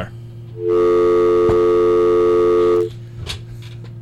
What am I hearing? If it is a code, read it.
Dial tone (North America), 350 + 440 Hz.